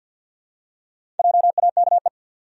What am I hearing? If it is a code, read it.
Morse code, heard as a keyed tone, about 52 wpm, 700 Hz. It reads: JAKE
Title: USA Common Names - Max 4 Letters - 50wpm